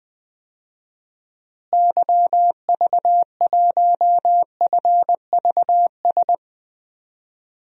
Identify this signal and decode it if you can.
Morse code, heard as a keyed tone, about 20 wpm, 700 Hz. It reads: YV1FVS
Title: DXCC-100 Call Signs - 6 Characters - Repeated 2x - 20wpm